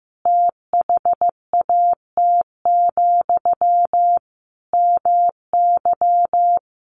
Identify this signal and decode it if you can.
Morse code, heard as a keyed tone, about 15 wpm, 700 Hz. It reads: THAT, MY
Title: Sentences from Top 1500 Words 15wpm